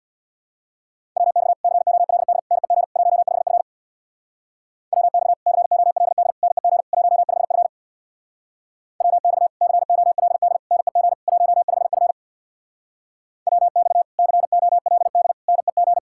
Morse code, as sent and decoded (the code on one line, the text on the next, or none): .-- -..- / -.-. --- .-.. -.. / -. --. / .---- ..... ..-. / .-- -..- / -.-. --- .-.. -.. / -. --. / .---- ..... ..-. / .-- -..- / -.-. --- .-.. -.. / -. --. / .---- ..... ..-. / .-- -..- / -.-. --- .-.. -.. / -. --.
WX COLD NEG 15F WX COLD NEG 15F WX COLD NEG 15F WX COLD NEG